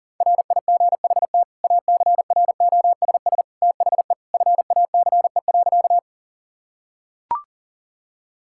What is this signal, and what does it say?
Morse code, heard as a keyed tone, about 40 wpm, 700 Hz. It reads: RIGHT ACROSS THE FACE.
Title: Sentences from Top 1000 Words - Review Part 5 - 40wpm